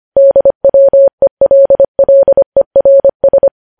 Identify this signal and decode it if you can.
Morse code, heard as a keyed tone, about 25 wpm, 550 Hz. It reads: DWELLERS
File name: podcast / News headlines-1x-25x25